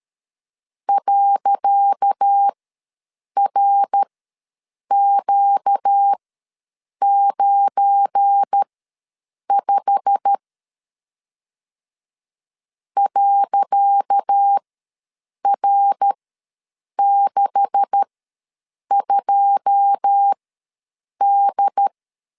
Morse code, heard as a keyed tone, about 13 wpm, spaced as for 7 wpm, 800 Hz. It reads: .RQ95 .R62D
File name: LEZIONI / 10 SIGNS / 10C DOT ws 12wpm 3548